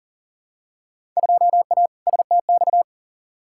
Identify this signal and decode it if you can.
Morse code, heard as a keyed tone, about 40 wpm, 700 Hz. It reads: 2A STX
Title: Field Day Exchanges - 40wpm